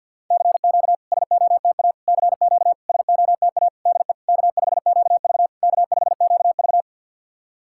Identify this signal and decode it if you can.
Morse code, heard as a keyed tone, about 52 wpm, 690 Hz. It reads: CQ SOTA CQ SOTA DE K5QV K5QV
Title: SOTA QSO Practice 50wpm